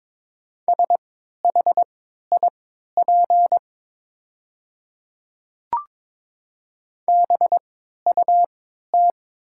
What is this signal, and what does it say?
Morse code, heard as a keyed tone, about 22 wpm, spaced as for 12 wpm, 700 Hz. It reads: SHIP BUT
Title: 3x Character Spacing - Top 500 Words - 22wpm